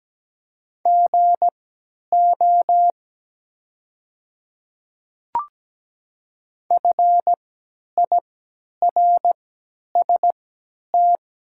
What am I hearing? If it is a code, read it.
Morse code, heard as a keyed tone, about 17 wpm, spaced as for 10 wpm, 700 Hz. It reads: GO FIRST
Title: 3x Character Spacing - Top 200 Words - 17wpm